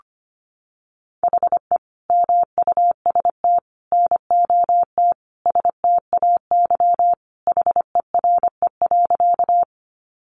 Morse code, heard as a keyed tone, about 25 wpm, 700 Hz. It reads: HE MUST NOT STAY HERE.
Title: Sentences from Top 600 Words 25wpm